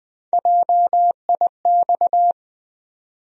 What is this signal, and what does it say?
Morse code, heard as a keyed tone, about 20 wpm, 700 Hz. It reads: JIX
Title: USA Call Sign Postfixes - 3 Letters 20wpm